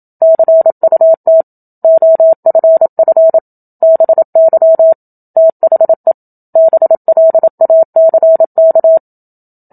Morse code, heard as a keyed tone, about 27 wpm, 650 Hz. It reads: CUT OFF BY THE BLACK
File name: Book Two / 01-Chapter 1 - 27 WPM